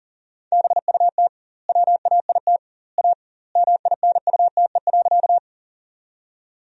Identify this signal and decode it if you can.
Morse code, heard as a keyed tone, about 40 wpm, 700 Hz. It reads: BUT WAIT A MINUTE.
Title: 2x Word-Spacing - Sentences from Top 500 Words - 40wpm